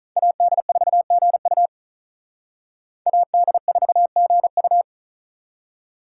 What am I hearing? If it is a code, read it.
Morse code, heard as a keyed tone, about 45 wpm, 700 Hz. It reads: AD4GU AD4GU